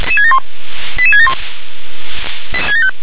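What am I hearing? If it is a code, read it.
Five-tone selective call (ZVEI), heard as quick runs of five tones, tones of about 70 ms. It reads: 88671
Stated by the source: ZVEI five-tone selective call 88773 received on SDR.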